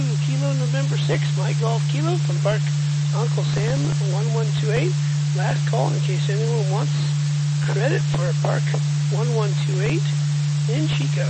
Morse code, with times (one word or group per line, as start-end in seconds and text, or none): none